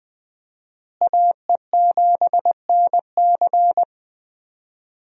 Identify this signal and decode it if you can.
Morse code, heard as a keyed tone, about 20 wpm, 700 Hz. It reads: AE7NC
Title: USA Call Signs - 5 Characters - Repeated 2x - 20wpm